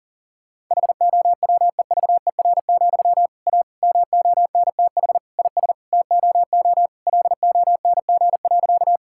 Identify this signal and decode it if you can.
Morse code, heard as a keyed tone, about 37 wpm, 700 Hz. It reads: HOWEVER, A MONTH IS TOO LONG.
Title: Sentences from Top 1000 Words - Review Part 5 - 40wpm